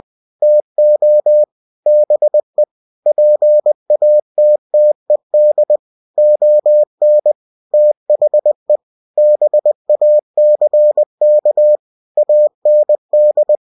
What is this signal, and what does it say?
Morse code, heard as a keyed tone, about 20 wpm, 600 Hz. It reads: TO BE PATTED ON THE BACK AND
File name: cw-books / winnie-the-pooh / winnie-20wpm-page-0107